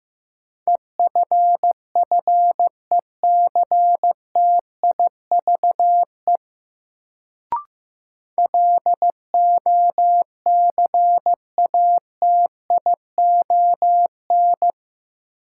Morse code, heard as a keyed tone, about 15 wpm, 700 Hz. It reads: EFFECTIVE LOCATION
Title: Sentences from Top 1000 Words - Review Part 6 - New Word Drill - 15wpm